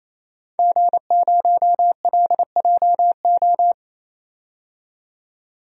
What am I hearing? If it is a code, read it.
Morse code, heard as a keyed tone, about 28 wpm, 700 Hz. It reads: G0LJO
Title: DXCC-100 Call Signs - 5 Characters - 28wpm